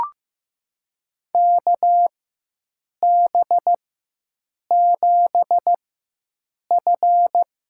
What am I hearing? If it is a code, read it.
Morse code, heard as a keyed tone, about 16 wpm, spaced as for 7 wpm, 700 Hz. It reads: KB7F
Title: USA Call Signs - 4 Characters - 4x Character Spacing - 15wpm